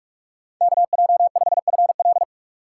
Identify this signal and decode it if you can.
Morse code, heard as a keyed tone, about 44 wpm, 700 Hz. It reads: KJ5FL